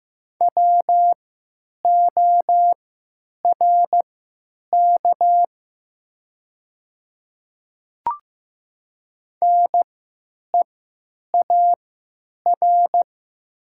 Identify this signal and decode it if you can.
Morse code, heard as a keyed tone, about 15 wpm, spaced as for 9 wpm, 700 Hz. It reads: WORK NEAR
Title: 3x Character Spacing - Top 200 Words - 15wpm